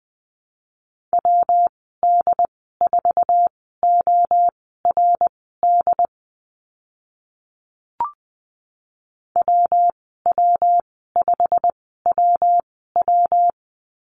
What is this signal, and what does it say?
Morse code, heard as a keyed tone, about 20 wpm, spaced as for 14 wpm, 700 Hz. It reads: WD4ORD WW5WW
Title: 122 Lesson 20g - Number 3 - Call Signs 20wpm